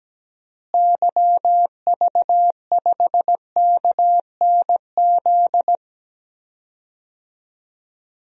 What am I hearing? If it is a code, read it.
Morse code, heard as a keyed tone, about 17 wpm, 700 Hz. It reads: YV5KNZ
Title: DXCC-100 Call Signs - 6 Characters - Repeated 3x - 17wpm